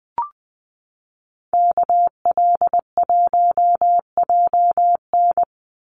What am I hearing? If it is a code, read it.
Morse code, heard as a keyed tone, about 20 wpm, 700 Hz. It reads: KL1JN
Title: USA Call Signs Repeated 3x 20wpm